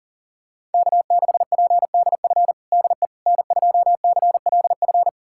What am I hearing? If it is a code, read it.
Morse code, heard as a keyed tone, about 40 wpm, 700 Hz. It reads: K6PDF DE N2CLF
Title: USA Callsign DE Callsign - 5 Characters - 40wpm